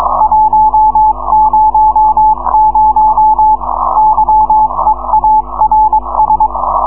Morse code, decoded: 100 YEARS